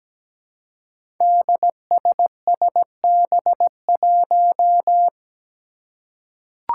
Morse code, heard as a keyed tone, about 17 wpm, 700 Hz. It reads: DSSB1
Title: Five Challenging Letter-Number Combinations 17wpm